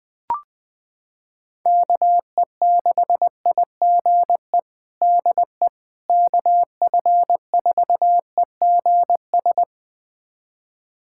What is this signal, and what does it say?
Morse code, heard as a keyed tone, about 21 wpm, 700 Hz. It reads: KE6IGE DE KF4EGS
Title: USA Callsign DE Callsign - 6 Characters - 20wpm